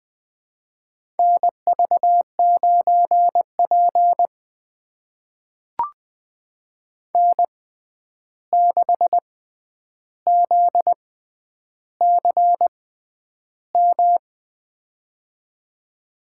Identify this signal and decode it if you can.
Morse code, heard as a keyed tone, about 21 wpm, spaced as for 10 wpm, 700 Hz. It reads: NV9P N6ZCM
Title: USA Call Signs - 6x Character Spacing - 20wpm